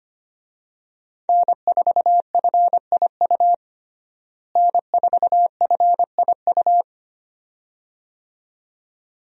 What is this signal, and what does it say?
Morse code, heard as a keyed tone, about 25 wpm, 700 Hz. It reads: N4FIU N4FIU
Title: USA Call Signs - 5 Characters - Repeated 2x - 25wpm